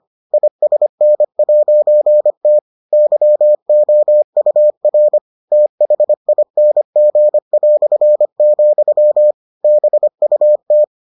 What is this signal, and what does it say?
Morse code, heard as a keyed tone, about 25 wpm, 600 Hz. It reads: ISN'T YOUR THING", BUT